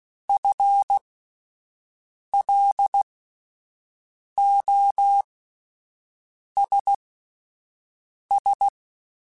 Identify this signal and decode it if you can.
Morse code, heard as a keyed tone, about 16 wpm, spaced as for 5 wpm, 800 Hz. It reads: FLOSS
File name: Alphabetical / Alphabetical mp3s / Fs @5